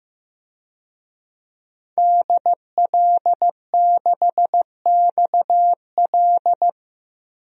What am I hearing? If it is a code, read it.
Morse code, heard as a keyed tone, about 15 wpm, 700 Hz. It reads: DL6XL
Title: DXCC-100 Call Signs - 5 Characters - 15wpm